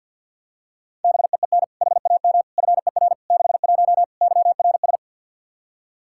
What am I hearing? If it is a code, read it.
Morse code, heard as a keyed tone, about 50 wpm, 700 Hz. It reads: BEEN HAM FER 61 YRS
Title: Practice QSO - Exchange 3 - More Info - 50wpm